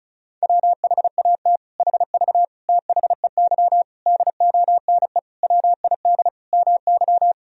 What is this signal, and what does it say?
Morse code, heard as a keyed tone, about 36 wpm, 700 Hz. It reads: WHAT HV THEY DONE WID MY
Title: Sentences from Top 500 Words and Abbreviations 35wpm